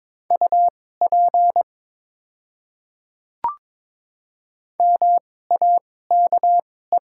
Morse code, heard as a keyed tone, about 22 wpm, spaced as for 16 wpm, 700 Hz. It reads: UP MAKE
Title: 2x Character Spacing - Top 100 Words - 22wpm